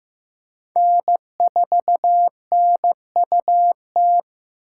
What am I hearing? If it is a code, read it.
Morse code, heard as a keyed tone, about 18 wpm, 700 Hz. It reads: N4NUT
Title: Speed Racing - USA 1x3 Call Signs 15wpm